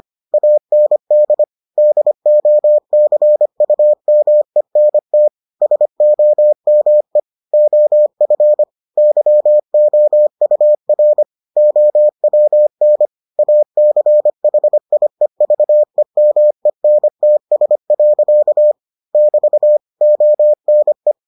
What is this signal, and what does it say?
Morse code, heard as a keyed tone, about 25 wpm, 600 Hz. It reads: AND DOCUMENT SOME OF YOUR OWN ACHIEVEMENTS. <BT> ONE